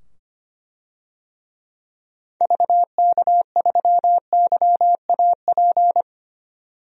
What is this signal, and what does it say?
Morse code, heard as a keyed tone, about 25 wpm, 700 Hz. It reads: VK3YAP
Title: DXCC-100 Call Signs - 6 Characters - 25wpm